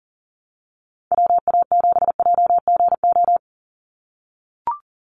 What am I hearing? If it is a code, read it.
Morse code, heard as a keyed tone, about 40 wpm, 700 Hz. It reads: WA7JGO
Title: USA Call Signs 40wpm